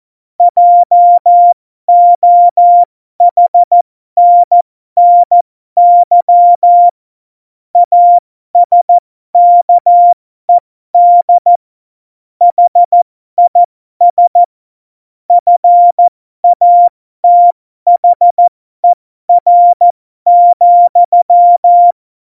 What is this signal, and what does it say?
Morse code, heard as a keyed tone, about 14 wpm, spaced as for 12 wpm, 700 Hz. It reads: JOHNNY ASKED HIS FATHER,
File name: Morsekurs / Witze